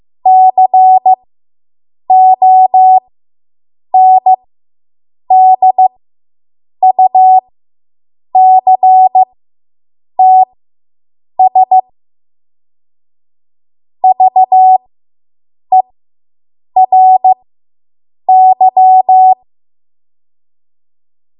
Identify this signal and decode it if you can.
Morse code, heard as a keyed tone, about 15 wpm, spaced as for 7 wpm, 750 Hz. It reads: CONDUCTS VERY